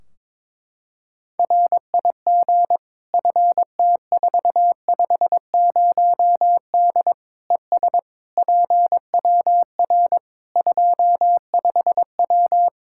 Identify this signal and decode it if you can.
Morse code, heard as a keyed tone, about 22 wpm, 700 Hz. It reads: RIG FT450D ES PWR 25W
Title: Practice QSO - Exchange 2 - Info - 22wpm